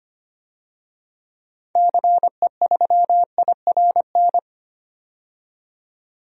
Morse code, heard as a keyed tone, about 25 wpm, 700 Hz. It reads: CE3IRN
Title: DXCC-100 Call Signs - 6 Characters - 25wpm